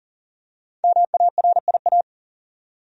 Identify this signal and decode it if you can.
Morse code, heard as a keyed tone, about 40 wpm, 700 Hz. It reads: MARIA
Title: USA Common Names - 40wpm